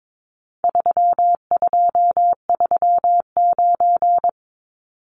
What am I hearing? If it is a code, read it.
Morse code, heard as a keyed tone, about 22 wpm, 700 Hz. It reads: 3239
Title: Numbers - 4-Digit 22wpm